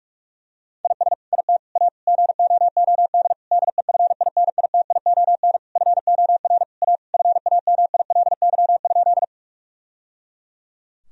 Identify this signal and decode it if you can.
Morse code, heard as a keyed tone, about 46 wpm, 700 Hz. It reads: IS IT A GOOD DEFINITION FOR A FAMILY?
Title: Sentences from Top 1000 Words - Review Part 4 - 45wpm